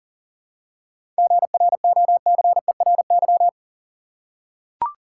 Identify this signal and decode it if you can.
Morse code, heard as a keyed tone, about 38 wpm, 700 Hz. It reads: GROCERY
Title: Seven Letter Words - 40wpm